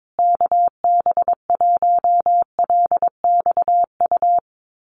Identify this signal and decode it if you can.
Morse code, heard as a keyed tone, about 22 wpm, 700 Hz. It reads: KB1LXU
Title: USA Call Signs Repeated 3x 22wpm